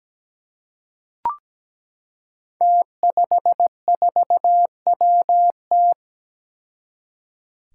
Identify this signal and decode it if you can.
Morse code, heard as a keyed tone, about 17 wpm, 700 Hz. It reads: T54WT
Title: Five Challenging Letter-Number Combinations 17wpm